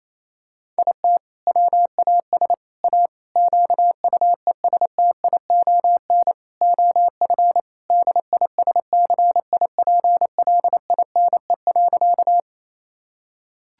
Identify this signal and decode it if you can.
Morse code, heard as a keyed tone, about 28 wpm, 700 Hz. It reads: IT WAS A QUESTION OF DISCIPLINE.
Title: Sentences from Top 1000 Words - Review Part 6 - 28wpm